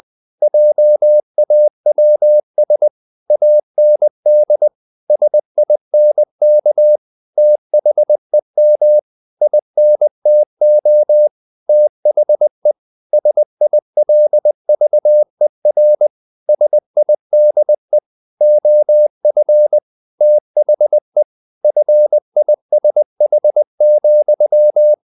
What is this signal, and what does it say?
Morse code, heard as a keyed tone, about 20 wpm, 600 Hz. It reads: JAWS AND SINK THEM INTO THE SILVER SIDE OF THE FISH,
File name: the old man and the sea part 2 20wpm